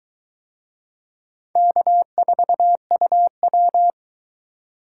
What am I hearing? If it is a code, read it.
Morse code, heard as a keyed tone, about 22 wpm, 700 Hz. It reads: K4UW